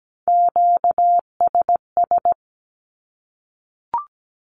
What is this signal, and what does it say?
Morse code, heard as a keyed tone, about 17 wpm, 700 Hz. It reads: QSS
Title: Q-Codes 17wpm